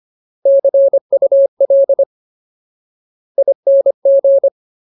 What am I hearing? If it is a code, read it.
Morse code, heard as a keyed tone, about 25 wpm, 550 Hz. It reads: CUL ING
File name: Ultimate Word List / 25WPM Ultimate